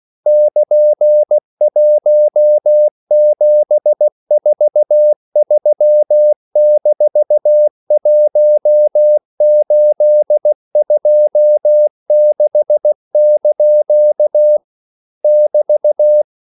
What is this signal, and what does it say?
Morse code, heard as a keyed tone, about 16 wpm, 600 Hz. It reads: <KN>1743-1826) <BT>
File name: CW / Practice / EnglishQuotations0116